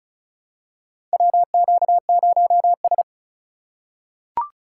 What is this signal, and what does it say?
Morse code, heard as a keyed tone, about 36 wpm, 700 Hz. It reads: WQ0S
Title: USA Call Signs - 4 Characters - 2x Character Spacing - 35wpm